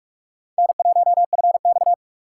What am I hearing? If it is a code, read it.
Morse code, heard as a keyed tone, about 41 wpm, 700 Hz. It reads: N1FX
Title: USA Call Signs - 4 Characters - 3x Character Spacing - 45wpm